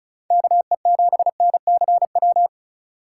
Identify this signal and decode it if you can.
Morse code, heard as a keyed tone, about 34 wpm, 700 Hz. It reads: KE7NCW
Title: USA Call Signs - 6 Characters - 2x Character Spacing - 35wpm